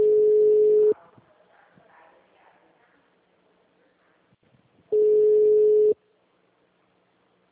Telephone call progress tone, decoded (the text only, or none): ringback tone